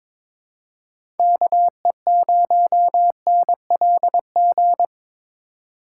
Morse code, heard as a keyed tone, about 21 wpm, 700 Hz. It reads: KE0NLG
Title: USA Call Signs - 6 Characters - 3x Character Spacing - 22wpm